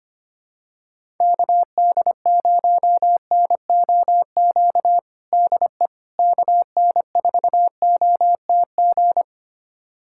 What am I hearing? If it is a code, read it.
Morse code, heard as a keyed tone, about 25 wpm, 700 Hz. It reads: KD0NOQ DE KN4OTG
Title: USA Callsign DE Callsign - 6 Characters - 25wpm